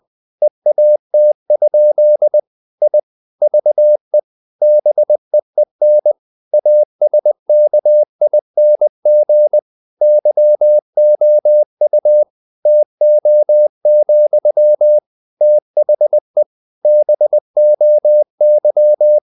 Morse code, heard as a keyed tone, about 20 wpm, 600 Hz. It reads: EAT? I VE BEEN ASKING YOU TO, THE BOY
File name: the old man and the sea / the old man and the sea part 1 20wpm